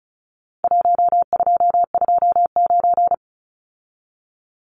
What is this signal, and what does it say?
Morse code, heard as a keyed tone, about 34 wpm, 700 Hz. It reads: 1229
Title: Numbers - Contest Style - 4-Digit 35wpm